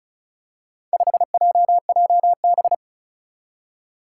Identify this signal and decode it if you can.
Morse code, heard as a keyed tone, about 36 wpm, 700 Hz. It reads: 5JJB
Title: Four Challenging Letter-Number Combinations 35wpm